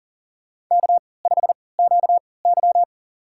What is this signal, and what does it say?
Morse code, heard as a keyed tone, about 40 wpm, spaced as for 23 wpm, 700 Hz. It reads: K5QY
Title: USA Call Signs - 4 Characters - 3x Character Spacing - 40wpm